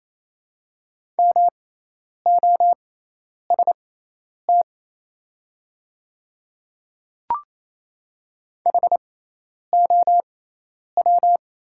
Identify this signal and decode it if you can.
Morse code, heard as a keyed tone, about 28 wpm, spaced as for 10 wpm, 700 Hz. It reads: MOST HOW
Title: 6x Character Spacing - Top 100 Words - 28wpm